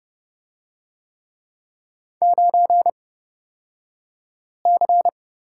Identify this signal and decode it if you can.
Morse code, heard as a keyed tone, about 29 wpm, spaced as for 5 wpm, 700 Hz. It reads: 9C
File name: Lesson 37d - Letter x - Two Characters 30wpm